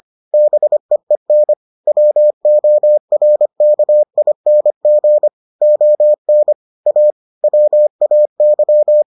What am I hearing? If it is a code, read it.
Morse code, heard as a keyed tone, about 25 wpm, 600 Hz. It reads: BEEN WORKING ON A WAY